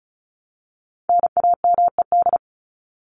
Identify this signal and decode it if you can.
Morse code, heard as a keyed tone, about 34 wpm, 700 Hz. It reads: NAMED
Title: ED Suffix - Top 100 Words with Base Words 35wpm